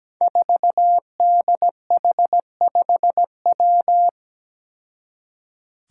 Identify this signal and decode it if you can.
Morse code, heard as a keyed tone, about 17 wpm, 700 Hz. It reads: 4DH5W
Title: Five Challenging Letter-Number Combinations 17wpm